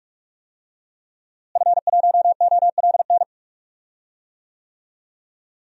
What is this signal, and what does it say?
Morse code, heard as a keyed tone, about 46 wpm, 700 Hz. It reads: F1OLN
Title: DXCC-100 Call Signs - 5 Characters - Repeated 3x - 45wpm